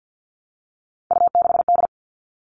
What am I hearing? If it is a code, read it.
Morse code, heard as a keyed tone, about 50 wpm, 700 Hz. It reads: U6D
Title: All Cut Numbers - 3-Digit 50wpm